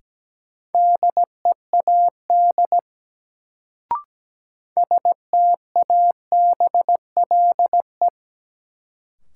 Morse code, heard as a keyed tone, about 17 wpm, 700 Hz. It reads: DEAD STABLE
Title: Sentences from Top 1500 Words - New Word Drill - 17wpm